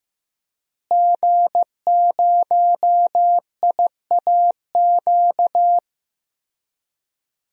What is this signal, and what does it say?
Morse code, heard as a keyed tone, about 15 wpm, 700 Hz. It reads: G0IAQ
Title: DXCC-100 Call Signs - 5 Characters - Repeated 3x - 15wpm